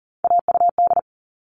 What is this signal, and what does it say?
Morse code, heard as a keyed tone, about 40 wpm, 700 Hz. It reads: AUD
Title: All Cut Numbers - 3-Digit 40wpm